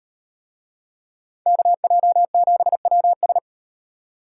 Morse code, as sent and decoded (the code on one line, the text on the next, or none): -.- .--- --... .-- ...
KJ7WS